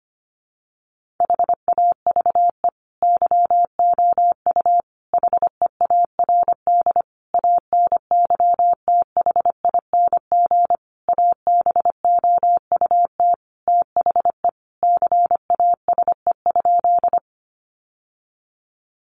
Morse code, heard as a keyed tone, about 25 wpm, 700 Hz. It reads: HAVE YOU HEARD ANYTHING ABOUT THE CASE?
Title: Sentences from Top 1000 Words 25wpm